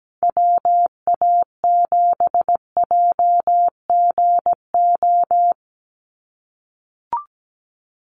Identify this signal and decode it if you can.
Morse code, heard as a keyed tone, about 17 wpm, 700 Hz. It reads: WA7JGO
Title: USA Call Signs 17wpm